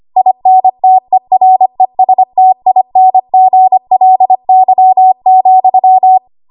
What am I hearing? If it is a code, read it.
Morse code, heard as a keyed tone, about 25 wpm, 750 Hz. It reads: INTERESTINGLY,
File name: Archive / 25 WPM / 25WPM